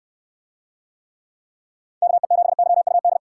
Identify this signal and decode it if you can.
Morse code, heard as a keyed tone, about 69 wpm, 700 Hz. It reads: KE7YUD